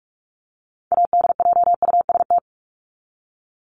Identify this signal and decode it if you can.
Morse code, heard as a keyed tone, about 44 wpm, 700 Hz. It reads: ADJUST
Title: Top 1501-1750 Words 45wpm